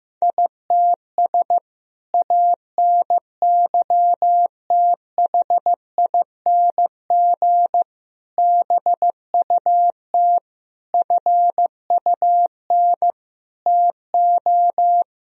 Morse code, heard as a keyed tone, about 15 wpm, 700 Hz. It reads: ITS ANYTHING BUT FUN TO
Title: Sentences from Top 1000 Words - Review Part 4 - 15wpm